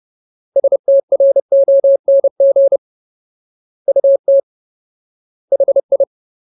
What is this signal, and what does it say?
Morse code, heard as a keyed tone, about 30 wpm, 550 Hz. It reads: STRONG UT HI
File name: Ultimate Word List / 30WPM Ultimate